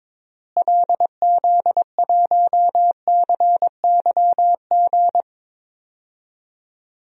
Morse code, heard as a keyed tone, about 22 wpm, 700 Hz. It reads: LZ1CYG
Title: DXCC-100 Call Signs - 6 Characters - Repeated 2x - 22wpm